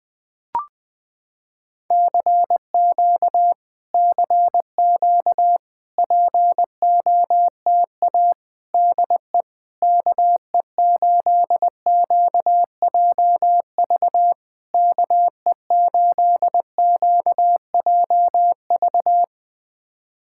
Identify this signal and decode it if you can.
Morse code, heard as a keyed tone, about 20 wpm, 700 Hz. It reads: CQ CQ POTA DE KE8QJV KE8QJV
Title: POTA QSO Practice - 20wpm